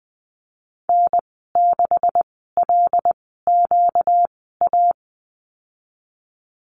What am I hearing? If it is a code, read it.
Morse code, heard as a keyed tone, about 20 wpm, spaced as for 14 wpm, 700 Hz. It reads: N6LQA
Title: 238 Lesson 38g - Letter q - Call Signs 20wpm